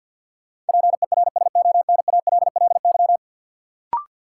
Morse code, heard as a keyed tone, about 53 wpm, 700 Hz. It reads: PERSONALLY
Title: Sentences from Top 1400 Words - New Word Drill - 50wpm